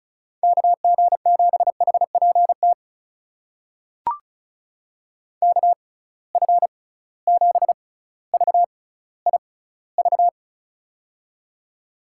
Morse code, heard as a keyed tone, about 36 wpm, spaced as for 18 wpm, 700 Hz. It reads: KG7HPT KF7VIV
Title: USA Call Signs - 6 Characters - 6x Character Spacing - 35wpm